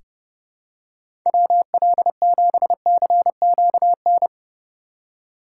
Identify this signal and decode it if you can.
Morse code, heard as a keyed tone, about 30 wpm, 700 Hz. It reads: WL7CQN
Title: USA Call Signs - 6 Characters - 4x Character Spacing - 30wpm